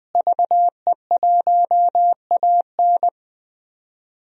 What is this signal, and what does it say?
Morse code, heard as a keyed tone, about 21 wpm, 700 Hz. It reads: VE1AN
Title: DXCC-100 Call Signs - 5 Characters - Repeated 2x - 20wpm